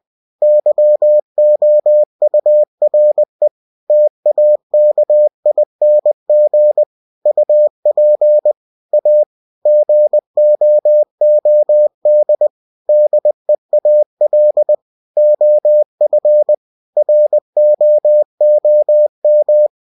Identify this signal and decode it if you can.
Morse code, heard as a keyed tone, about 20 wpm, 600 Hz. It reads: YOURE TAKING UP A GOOD DEAL OF ROOM
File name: cw-books / winnie-the-pooh / winnie-20wpm-page-0019